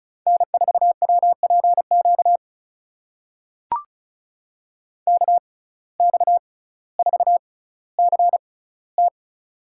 Morse code, heard as a keyed tone, about 36 wpm, spaced as for 18 wpm, 700 Hz. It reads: N4WPQ KX4CT
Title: USA Call Signs - 5 Characters - 6x Character Spacing - 35wpm